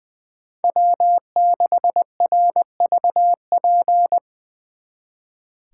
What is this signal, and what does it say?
Morse code, heard as a keyed tone, about 23 wpm, 700 Hz. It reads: W6RVP
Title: Speed Racing - USA Call Signs 20wpm